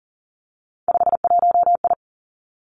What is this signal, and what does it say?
Morse code, heard as a keyed tone, about 40 wpm, 700 Hz. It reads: 51I
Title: Three Challenging Letter-Number Combinations 40wpm